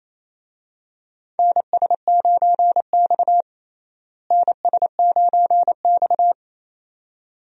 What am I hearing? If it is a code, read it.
Morse code, heard as a keyed tone, about 28 wpm, 700 Hz. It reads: NS9X NS9X
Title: USA Call Signs - 4 Characters - Repeated 2x - 28wpm